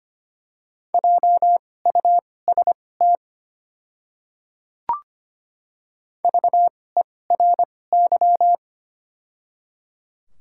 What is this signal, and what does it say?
Morse code, heard as a keyed tone, about 25 wpm, spaced as for 18 wpm, 700 Hz. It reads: JUST VERY
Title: 2x Character Spacing - Top 200 Words - 25wpm